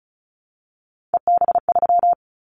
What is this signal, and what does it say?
Morse code, heard as a keyed tone, about 34 wpm, 700 Hz. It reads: EB3
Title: Callsign Prefixes for Top 25 DXCC - Repeated 3x - 35wpm